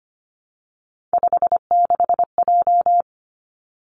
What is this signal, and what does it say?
Morse code, heard as a keyed tone, about 25 wpm, 700 Hz. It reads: 56J
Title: Three Challenging Letter-Number Combinations 25wpm